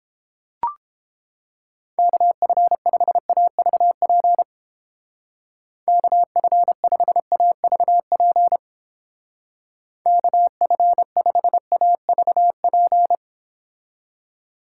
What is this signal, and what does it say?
Morse code, heard as a keyed tone, about 29 wpm, 700 Hz. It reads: KF5AVP KF5AVP KF5AVP